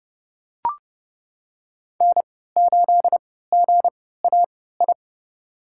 Morse code, heard as a keyed tone, about 32 wpm, spaced as for 17 wpm, 700 Hz. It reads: N8GAI
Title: USA Call Signs - 5 Characters - 3x Character Spacing - 30wpm